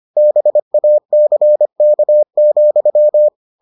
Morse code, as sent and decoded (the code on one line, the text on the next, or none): -... .- -.-. -.- --..--
BACK,